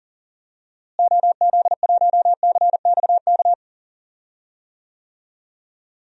Morse code, heard as a keyed tone, about 40 wpm, 700 Hz. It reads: OZ1CXK
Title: DXCC-100 Call Signs - 6 Characters - Repeated 3x - 40wpm